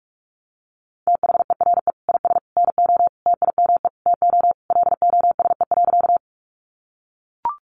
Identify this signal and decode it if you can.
Morse code, heard as a keyed tone, about 44 wpm, 700 Hz. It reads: THERE IS NO TIME TO LOSE.
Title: Sentences from Top 1100 Words 45wpm